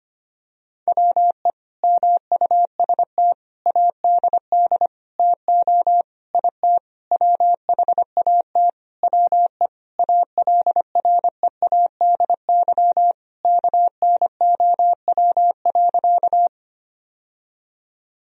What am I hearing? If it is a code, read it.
Morse code, heard as a keyed tone, about 25 wpm, 700 Hz. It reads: WE MUST ADD TO IT WHAT WE ALREADY KNOW.
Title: Sentences from Top 1000 Words - Review Part 1 - 25wpm